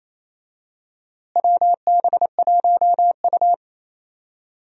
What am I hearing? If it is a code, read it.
Morse code, heard as a keyed tone, about 28 wpm, 700 Hz. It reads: WB1U
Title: USA Call Signs - 4 Characters - Repeated 3x - 28wpm